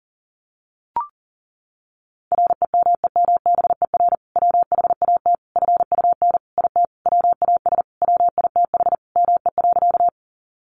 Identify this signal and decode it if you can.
Morse code, heard as a keyed tone, about 40 wpm, 700 Hz. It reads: REMEMBER WHAT FUN IT WAS WITH ME.
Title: Sentences from Top 700 Words 40wpm